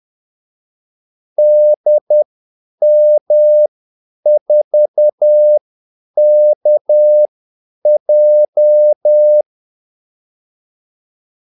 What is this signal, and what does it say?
Morse code, heard as a keyed tone, about 10 wpm, spaced as for 8 wpm, 600 Hz. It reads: DM4KJ
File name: Callsigns-10-8wpm